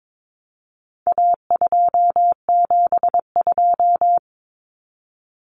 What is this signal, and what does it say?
Morse code, heard as a keyed tone, about 22 wpm, 700 Hz. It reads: A272
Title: Common Cut Numbers - 4-Digit 22wpm